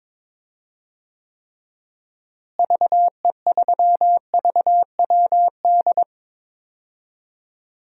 Morse code, heard as a keyed tone, about 22 wpm, 700 Hz. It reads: VE3VWD
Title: DXCC-100 Call Signs - 6 Characters - Repeated 3x - 22wpm